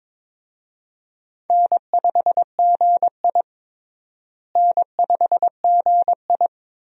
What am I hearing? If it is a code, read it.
Morse code, heard as a keyed tone, about 21 wpm, 700 Hz. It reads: N5GI N5GI
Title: USA Call Signs - 4 Characters - Repeated 2x - 22wpm